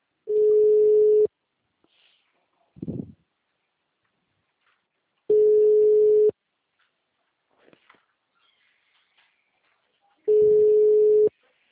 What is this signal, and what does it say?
Ringback tone (Europe, 425 Hz).